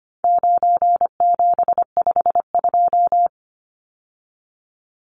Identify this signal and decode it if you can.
Morse code, heard as a keyed tone, about 25 wpm, 700 Hz. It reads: 9752
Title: Numbers - 4-Digit 25wpm